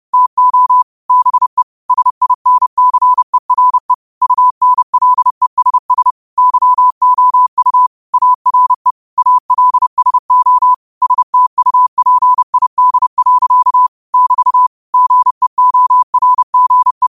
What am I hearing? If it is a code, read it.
Morse code, heard as a keyed tone, about 30 wpm, 1000 Hz. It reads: TO BE SINCERE UNLESS YOU ARE ALSO STUPID. <BT> GEORGE
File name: qotdfiles / 30wpm